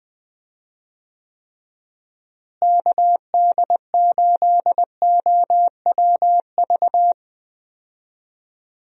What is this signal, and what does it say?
Morse code, heard as a keyed tone, about 20 wpm, 700 Hz. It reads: KD8OWV
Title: USA Call Signs - 6 Characters - Repeated 2x - 20wpm